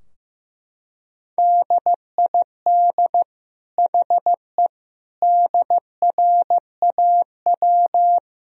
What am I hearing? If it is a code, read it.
Morse code, heard as a keyed tone, about 15 wpm, 700 Hz. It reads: DID HE DRAW
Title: Sentences from Top 1000 Words - Review Part 2 - 15wpm